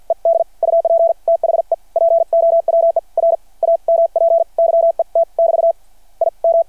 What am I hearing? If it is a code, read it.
Morse code, heard as a keyed tone, about 53 wpm, spaced as for 44 wpm, 660 Hz. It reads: ED UP THE POWER AMP YET <BT> IM
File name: archived / CRSnet